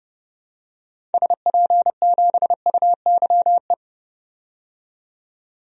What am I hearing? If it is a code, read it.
Morse code, heard as a keyed tone, about 30 wpm, 700 Hz. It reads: SP7UYE
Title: DXCC-100 Call Signs - 6 Characters - 30wpm